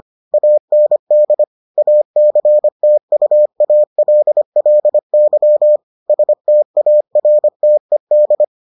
Morse code, heard as a keyed tone, about 25 wpm, 600 Hz. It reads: AND ACTUALLY STARTED